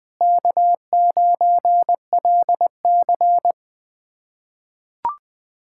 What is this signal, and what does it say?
Morse code, heard as a keyed tone, about 20 wpm, 700 Hz. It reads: K9LC
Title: K1USN SST QSO Practice - 20wpm